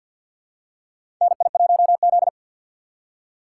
Morse code, heard as a keyed tone, about 53 wpm, 690 Hz. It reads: NI1Z
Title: USA Call Signs - 4x Character Spacing - 50wpm